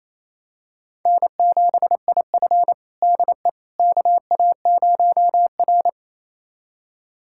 Morse code, heard as a keyed tone, about 28 wpm, 700 Hz. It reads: N7IF DE KA0R
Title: USA Callsign DE Callsign - 4 Characters - 28wpm